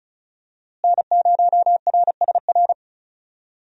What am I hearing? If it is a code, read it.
Morse code, heard as a keyed tone, about 36 wpm, 700 Hz. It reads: N0RSR
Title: USA Call Signs - 5 Characters - Repeated 2x - 35wpm